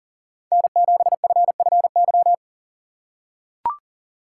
Morse code, heard as a keyed tone, about 40 wpm, 700 Hz. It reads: N7FFY
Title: USA Call Signs - 5 Characters - 6x Character Spacing - 40wpm